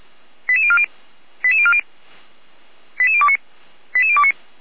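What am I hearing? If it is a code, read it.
Five-tone selective call (ZVEI), heard as quick runs of five tones, tones of about 70 ms. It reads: 88040 88040 88029 88029
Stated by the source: ZVEI five-tone call received on SDR, from the feuerwehr-orlamuende.org fire brigade server.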